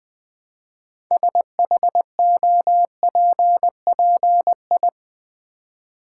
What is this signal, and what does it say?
Morse code, heard as a keyed tone, about 20 wpm, 700 Hz. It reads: SHOPPI
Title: Build A Word - 3 Letters At A Time - 8-Letter Words - 20wpm